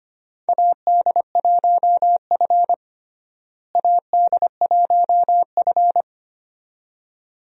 Morse code, heard as a keyed tone, about 25 wpm, 700 Hz. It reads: AD1F AD1F
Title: USA Call Signs - 4 Characters - Repeated 2x - 25wpm